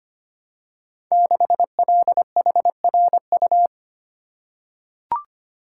Morse code, heard as a keyed tone, about 25 wpm, 690 Hz. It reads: 6LHRU
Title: Code Groups - 5 Letter:Numbers - 25wpm